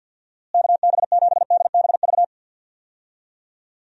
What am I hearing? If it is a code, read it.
Morse code, heard as a keyed tone, about 50 wpm, 700 Hz. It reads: KB7DBV
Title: USA Call Signs - 6 Characters - Repeated 2x - 50wpm